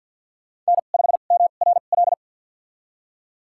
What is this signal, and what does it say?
Morse code, heard as a keyed tone, about 53 wpm, spaced as for 37 wpm, 700 Hz. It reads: N5MRL
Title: USA Call Signs - 5 Characters - 2x Character Spacing - 50wpm